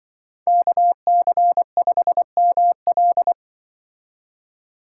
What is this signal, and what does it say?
Morse code, heard as a keyed tone, about 24 wpm, 700 Hz. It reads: KC5ML
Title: Speed Racing - USA Call Signs 20wpm